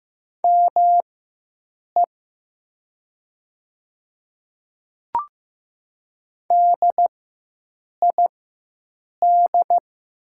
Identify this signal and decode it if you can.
Morse code, heard as a keyed tone, about 15 wpm, spaced as for 7 wpm, 700 Hz. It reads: ME DID
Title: 4x Character Spacing - Top 200 Words - 15wpm